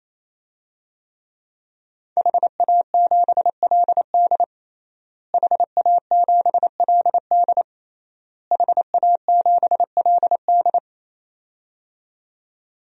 Morse code, heard as a keyed tone, about 27 wpm, 700 Hz. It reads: HA7LD HA7LD HA7LD
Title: DXCC-100 Call Signs - 5 Characters - Repeated 3x - 28wpm